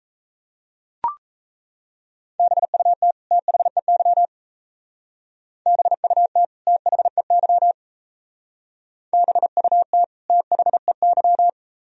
Morse code, heard as a keyed tone, about 36 wpm, 700 Hz. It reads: BUT THEY BUT THEY BUT THEY